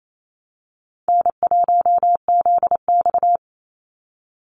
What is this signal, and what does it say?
Morse code, heard as a keyed tone, about 28 wpm, 700 Hz. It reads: N1ZX
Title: Easy Call Signs Repeated 3x - 28wpm